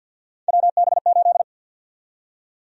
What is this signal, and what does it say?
Morse code, heard as a keyed tone, about 47 wpm, 700 Hz. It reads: WB8
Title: USA Call Sign Prefixes - Repeated 2x - 50wpm